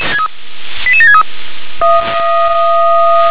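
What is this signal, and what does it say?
Five-tone selective call (ZVEI), heard as quick runs of five tones, tones of about 70 ms. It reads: 88763
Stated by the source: ZVEI five-tone selective call received by SDR.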